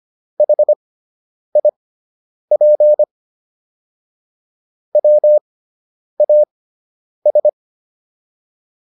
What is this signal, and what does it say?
Morse code, heard as a keyed tone, about 25 wpm, spaced as for 9 wpm, 600 Hz. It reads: HIP WAS